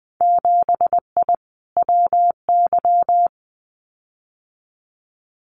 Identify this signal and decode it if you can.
Morse code, heard as a keyed tone, about 20 wpm, 700 Hz. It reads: 7I WY
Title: Winter Field Day Exchanges Encore 20wpm